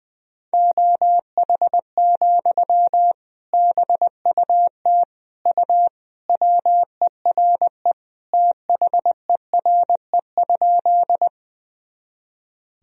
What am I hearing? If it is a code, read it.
Morse code, heard as a keyed tone, about 20 wpm, 700 Hz. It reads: OH, BUT U WERE THERE?
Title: Sentences from Top 500 Words and Abbreviations 20wpm